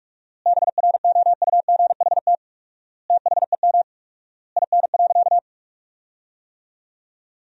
Morse code, heard as a keyed tone, about 47 wpm, 700 Hz. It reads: BROUGHT THEM IN.
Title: 4x Word-Spacing - Sentences from Top 500 Words - 45wpm